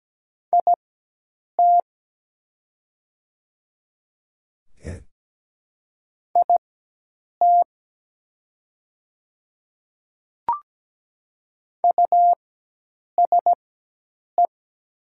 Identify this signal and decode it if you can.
Morse code, heard as a keyed tone, about 17 wpm, spaced as for 8 wpm, 700 Hz. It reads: IT IT USE
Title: 4x Character Spacing - Top 100 Words - 17wpm